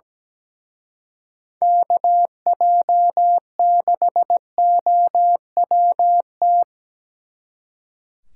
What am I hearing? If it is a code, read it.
Morse code, heard as a keyed tone, about 20 wpm, 700 Hz. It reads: KJ6OWT
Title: Speed Racing - USA 2x3 Call Signs 17wpm